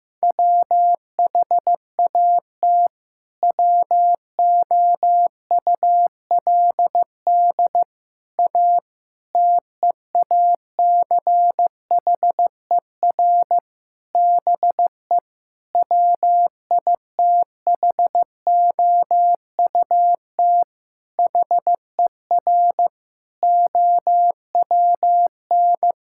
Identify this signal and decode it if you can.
Morse code, heard as a keyed tone, about 15 wpm, 700 Hz. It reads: WHAT WOULD A TEACHER BE WITHOUT HER OWN
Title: Sentences from Top 1000 Words - Review Part 6 - 15wpm